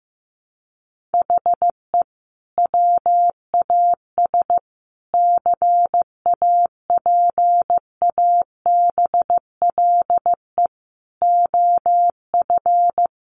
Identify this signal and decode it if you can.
Morse code, heard as a keyed tone, about 15 wpm, 700 Hz. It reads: HE WAS CAPABLE OF
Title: Sentences from Top 1200 Words 15wpm